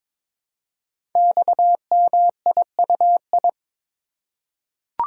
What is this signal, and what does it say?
Morse code, heard as a keyed tone, about 22 wpm, 700 Hz. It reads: XMIUI
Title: Code Groups - 5 Letters - 22wpm